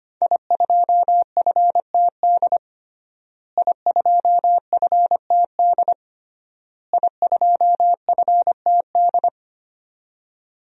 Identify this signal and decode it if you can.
Morse code, heard as a keyed tone, about 25 wpm, 700 Hz. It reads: I2FTD I2FTD I2FTD
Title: DXCC-100 Call Signs - 5 Characters - Repeated 3x - 25wpm